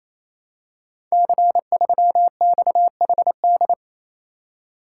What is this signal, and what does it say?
Morse code, heard as a keyed tone, about 29 wpm, 700 Hz. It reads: C3XHD
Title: Code Groups - 5 Letter:Numbers - 28wpm